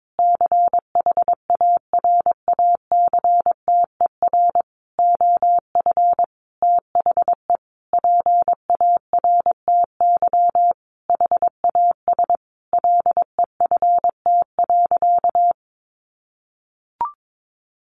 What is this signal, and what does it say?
Morse code, heard as a keyed tone, about 22 wpm, 700 Hz. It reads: CHARACTER OF THE PARTY HAS LEFT.
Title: Sentences from Top 1500 Words 22wpm